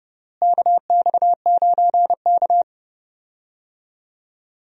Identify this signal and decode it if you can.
Morse code, heard as a keyed tone, about 32 wpm, 700 Hz. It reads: KX9K
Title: USA Call Signs - 4 Characters - Repeated 2x - 30wpm